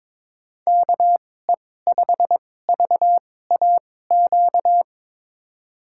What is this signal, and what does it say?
Morse code, heard as a keyed tone, about 22 wpm, spaced as for 16 wpm, 700 Hz. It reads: KE5VAQ
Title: USA Call Signs - 6 Characters - 2x Character Spacing - 22wpm